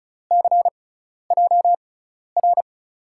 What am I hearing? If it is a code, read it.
Morse code, heard as a keyed tone, about 34 wpm, spaced as for 12 wpm, 700 Hz. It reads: CJR
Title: USA Call Signs - 5 Characters - 6x Character Spacing - 35wpm